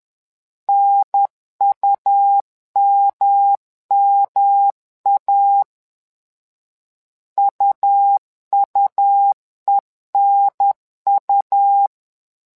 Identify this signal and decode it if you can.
Morse code, heard as a keyed tone, about 10 wpm, 800 Hz. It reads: NUMMA UUENU
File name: LEZIONI / 3 MWRO / 3 MWRO 10wpm 2148